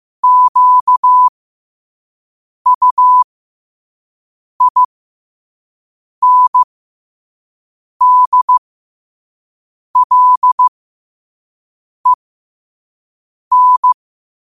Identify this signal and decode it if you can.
Morse code, heard as a keyed tone, about 15 wpm, spaced as for 5 wpm, 1000 Hz. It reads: QUINDLEN
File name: qotdfiles / 5wpm